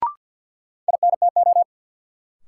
Morse code, heard as a keyed tone, about 53 wpm, 700 Hz. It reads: INTO
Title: Sentences from Top 600 Words - New Word Drill - 50wpm